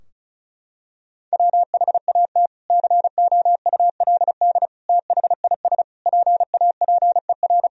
Morse code, heard as a keyed tone, about 34 wpm, 700 Hz. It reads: WHAT COULD THIS PAPER
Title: Sentences from Top 1000 Words - Review Part 2 - 35wpm